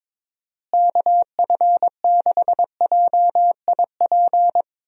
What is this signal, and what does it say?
Morse code, heard as a keyed tone, about 22 wpm, 700 Hz. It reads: KF6JIP
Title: USA Call Signs - 6 Characters - Repeated 3x - 22wpm